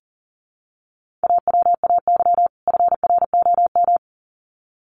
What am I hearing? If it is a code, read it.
Morse code, heard as a keyed tone, about 40 wpm, 700 Hz. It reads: AWAY FROM
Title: Sets of 2 Words 40wpm